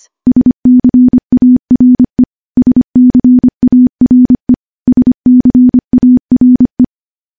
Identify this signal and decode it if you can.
Morse code, heard as a keyed tone, about 25 wpm, 250 Hz. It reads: SCARE SCARE SCARE
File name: morse-training-25WPM